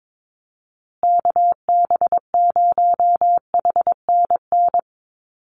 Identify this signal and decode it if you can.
Morse code, heard as a keyed tone, about 22 wpm, 700 Hz. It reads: KB0HNN
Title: USA Call Signs Repeated 2x 22wpm